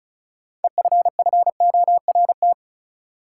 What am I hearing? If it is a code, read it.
Morse code, heard as a keyed tone, about 34 wpm, 700 Hz. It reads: EFFORT
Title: Sentences from Top 1200 Words - New Word Drill - 35wpm